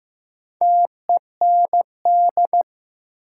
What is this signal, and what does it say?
Morse code, heard as a keyed tone, about 15 wpm, 700 Hz. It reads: TEND
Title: Sentences from Top 1000 Words - Review Part 5 - New Word Drill - 15wpm